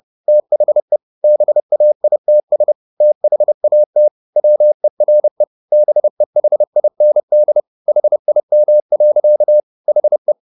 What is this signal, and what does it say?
Morse code, heard as a keyed tone, about 30 wpm, 600 Hz. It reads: THE BAITS THAT WERE BEHIND HIM. HE
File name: the old man and the sea / the old man and the sea 1 30wpm